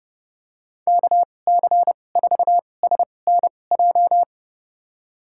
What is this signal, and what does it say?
Morse code, heard as a keyed tone, about 30 wpm, spaced as for 22 wpm, 700 Hz. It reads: KC4SNJ
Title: USA Call Signs - 6 Characters - 2x Character Spacing - 30wpm